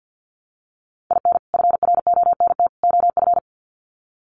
Morse code, heard as a keyed tone, about 50 wpm, 700 Hz. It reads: IN FRONT OF
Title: Sets of 3 Words 50wpm